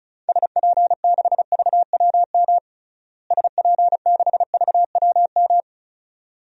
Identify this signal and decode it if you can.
Morse code, heard as a keyed tone, about 36 wpm, 700 Hz. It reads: SP6VWM SP6VWM
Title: DXCC-100 Call Signs - 6 Characters - Repeated 2x - 35wpm